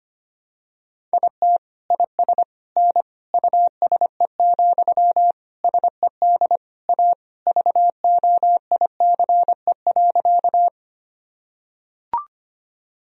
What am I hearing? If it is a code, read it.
Morse code, heard as a keyed tone, about 25 wpm, 700 Hz. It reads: IT IS N USE, SED A VOICE.
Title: Sentences from Top 500 Words and Abbreviations 25wpm